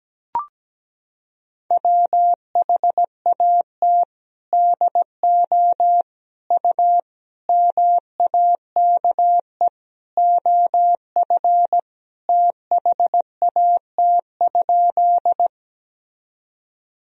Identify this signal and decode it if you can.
Morse code, heard as a keyed tone, about 17 wpm, 700 Hz. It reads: WHAT DO U MAKE OF THAT?
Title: Sentences from Top 100 Words and Abbreviations 17wpm